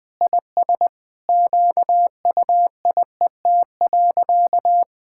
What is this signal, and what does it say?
Morse code, heard as a keyed tone, about 20 wpm, 700 Hz. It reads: IS QUIET.
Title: Sentences with 2-Word Phrases - 20wpm